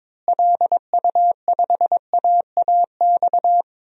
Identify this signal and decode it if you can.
Morse code, heard as a keyed tone, about 22 wpm, 700 Hz. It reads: LU5AAX
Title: DXCC-100 Call Signs - 6 Characters - Repeated 3x - 22wpm